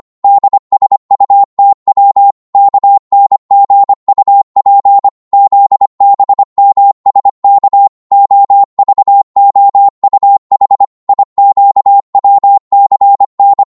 Morse code, heard as a keyed tone, about 25 wpm, 800 Hz. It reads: DSUTW KNGUP ZBMSK OVOUH IQWCN
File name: Morse / ltr-25wpm